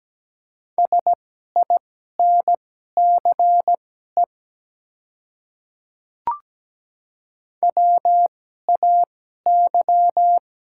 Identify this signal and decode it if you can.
Morse code, heard as a keyed tone, about 17 wpm, spaced as for 12 wpm, 690 Hz. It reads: SINCE WAY
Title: 2x Character Spacing - Top 300 Words - 17wpm